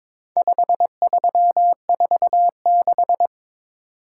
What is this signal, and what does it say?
Morse code, heard as a keyed tone, about 22 wpm, 700 Hz. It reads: 5346
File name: practice / 22wpm / Four Letter-Number Combinations - 22wpm